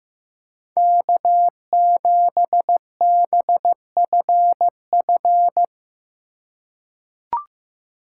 Morse code, heard as a keyed tone, about 15 wpm, 700 Hz. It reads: K7BFF
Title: USA Call Signs - 4x Character Spacing - 15wpm